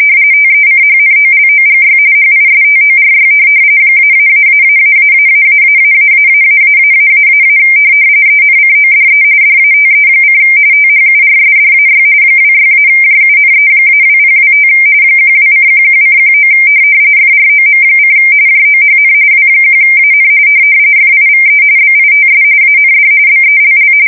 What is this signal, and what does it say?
RTTY, about 45.45 baud. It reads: TECHNOLOGY HISTORY ---------------------- ENCODE DATE: THURSDAY 26-JUNE-2025 12:11:19 FROM: JOHN W1AN FIVE BITS AND BELL
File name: fsk170 / printing-telegraph